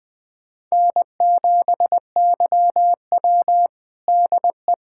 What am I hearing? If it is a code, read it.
Morse code, heard as a keyed tone, about 21 wpm, 700 Hz. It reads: N7YW DE
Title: USA Callsign DE Callsign - 4 Characters - 20wpm